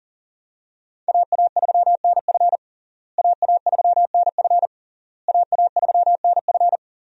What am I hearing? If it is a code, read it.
Morse code, heard as a keyed tone, about 40 wpm, 700 Hz. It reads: AA3NF AA3NF AA3NF
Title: USA Call Signs - 5 Characters - Repeated 3x - 40wpm